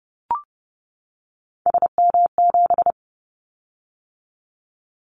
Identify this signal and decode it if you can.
Morse code, heard as a keyed tone, about 30 wpm, 700 Hz. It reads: SM7
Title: Callsign Prefixes for Top 25 DXCC - 30wpm